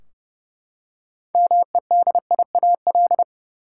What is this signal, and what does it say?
Morse code, heard as a keyed tone, about 31 wpm, 700 Hz. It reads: MEDIAL
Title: Practice QSO - Exchange 3 - More Info - 30wpm